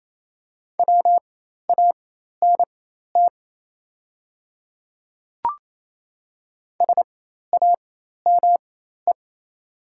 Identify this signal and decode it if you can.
Morse code, heard as a keyed tone, about 28 wpm, spaced as for 13 wpm, 700 Hz. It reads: WANT SAME
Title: 4x Character Spacing - Top 200 Words - 28wpm